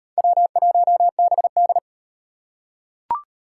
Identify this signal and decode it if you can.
Morse code, heard as a keyed tone, about 38 wpm, 700 Hz. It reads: W1BD